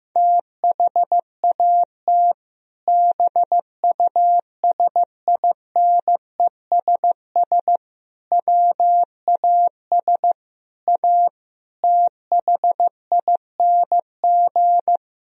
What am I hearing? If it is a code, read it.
Morse code, heard as a keyed tone, about 15 wpm, 700 Hz. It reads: THAT BUSINESS WAS A THING
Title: Sentences from Top 1000 Words - Review Part 3 - 15wpm